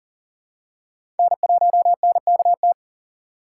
Morse code, heard as a keyed tone, about 40 wpm, 700 Hz. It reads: N1NKT
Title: USA Call Signs - 5 Characters - 2x Character Spacing - 40wpm